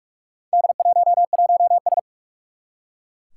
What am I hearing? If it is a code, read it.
Morse code, heard as a keyed tone, about 41 wpm, 700 Hz. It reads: D11S
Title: Four Challenging Letter-Number Combinations 45wpm